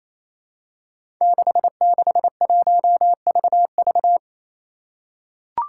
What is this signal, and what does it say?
Morse code, heard as a keyed tone, about 28 wpm, 700 Hz. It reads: 661VV
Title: Five Challenging Letter-Number Combinations 28wpm